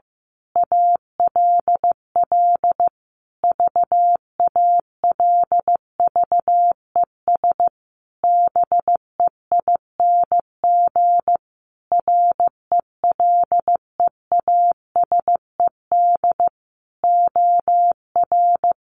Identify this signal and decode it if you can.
Morse code, heard as a keyed tone, about 15 wpm, 700 Hz. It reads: ALL VALVES BEING RELEASED OR
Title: A Sound Of Thunder 15wpm